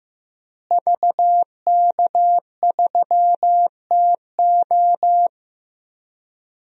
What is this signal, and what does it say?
Morse code, heard as a keyed tone, about 15 wpm, 700 Hz. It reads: VK3TO
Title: DXCC-100 Call Signs - 5 Characters - Repeated 3x - 15wpm